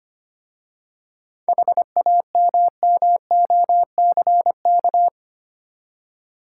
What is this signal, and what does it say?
Morse code, heard as a keyed tone, about 25 wpm, 690 Hz. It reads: HAMMOCK
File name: practice / 25wpm / Easy Words with Hints - 25wpm